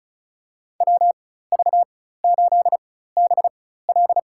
Morse code, heard as a keyed tone, about 36 wpm, spaced as for 16 wpm, 700 Hz. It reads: WV8BL
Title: USA Call Signs - 4x Character Spacing - 35wpm